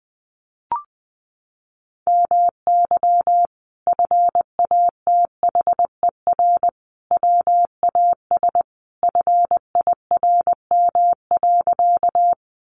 Morse code, heard as a keyed tone, about 20 wpm, 700 Hz. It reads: MY FATHER WAS FIRM.
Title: Sentences from Top 1400 Words 20wpm